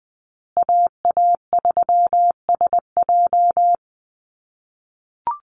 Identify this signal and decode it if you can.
Morse code, heard as a keyed tone, about 20 wpm, 700 Hz. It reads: AA3SJ
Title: USA Call Signs Repeated 3x 20wpm